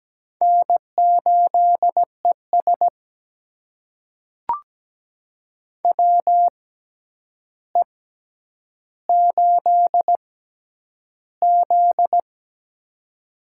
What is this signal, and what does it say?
Morse code, heard as a keyed tone, about 17 wpm, spaced as for 9 wpm, 700 Hz. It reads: N8ES WE8Z
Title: USA Call Signs - 4 Characters - 6x Character Spacing - 17wpm